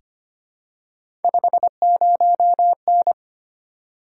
Morse code, heard as a keyed tone, about 25 wpm, 700 Hz. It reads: 50N
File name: Three Letter-Number Combinations - 25wpm